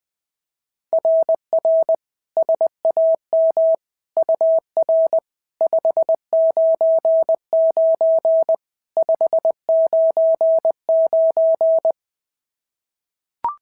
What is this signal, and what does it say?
Morse code, heard as a keyed tone, about 20 wpm, 690 Hz. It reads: RR SAM UR 599 599
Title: SOTA QSO Practice 20wpm